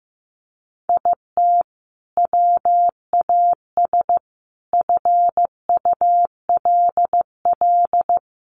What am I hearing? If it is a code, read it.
Morse code, heard as a keyed tone, about 15 wpm, 700 Hz. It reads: IT WAS FULL
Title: Sentences from Top 500 Words 15wpm